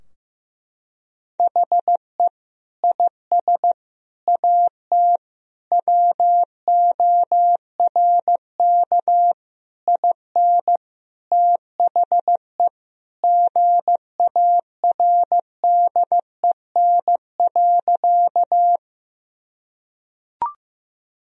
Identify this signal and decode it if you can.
Morse code, heard as a keyed tone, about 15 wpm, 700 Hz. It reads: HE IS AT WORK IN THE GARDEN.
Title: Sentences from Top 1000 Words - Review Part 4 - 15wpm